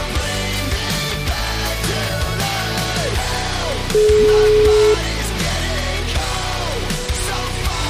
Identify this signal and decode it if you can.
Ringback tone (Europe, 425 Hz), with noise.